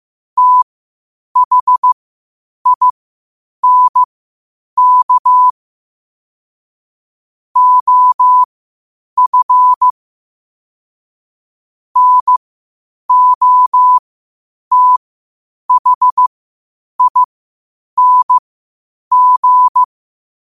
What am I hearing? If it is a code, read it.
Morse code, heard as a keyed tone, about 15 wpm, spaced as for 8 wpm, 1000 Hz. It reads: THINK OF NOTHING
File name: qotdfiles / 8wpm